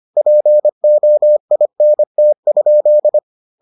Morse code, heard as a keyed tone, about 25 wpm, 600 Hz. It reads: POINT?